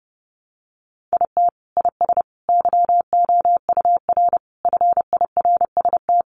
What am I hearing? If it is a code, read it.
Morse code, heard as a keyed tone, about 30 wpm, 700 Hz. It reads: IT IS YOUR FIRST
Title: Sentences from Top 1100 Words 30wpm